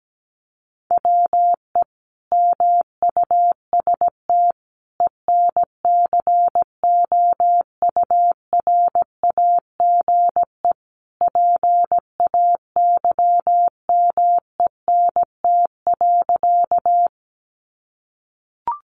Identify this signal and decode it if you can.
Morse code, heard as a keyed tone, about 17 wpm, 700 Hz. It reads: WE MUST ENCOURAGE PAYMENT.
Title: Sentences from Top 1400 Words 17wpm